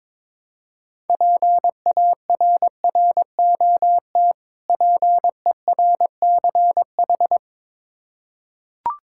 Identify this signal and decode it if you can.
Morse code, heard as a keyed tone, about 22 wpm, 690 Hz. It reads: PARROT PERCH
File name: Difficult Words with Hints - 22wpm